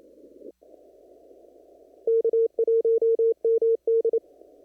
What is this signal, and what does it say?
Morse code, heard as a keyed tone, about 28 wpm, 460 Hz. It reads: K1MD
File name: CW / kt5j wpx cw-op-k5pi